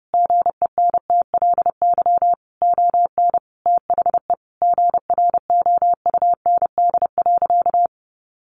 Morse code, heard as a keyed tone, about 30 wpm, 700 Hz. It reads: GENTLY ON THE GROUND.
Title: Sentences from Top 1500 Words 30wpm